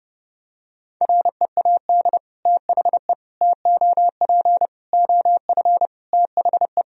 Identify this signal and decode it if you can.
Morse code, heard as a keyed tone, about 30 wpm, 700 Hz. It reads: READ THE TOP OF THE
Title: Sentences from Top 1000 Words - Review Part 2 - 30wpm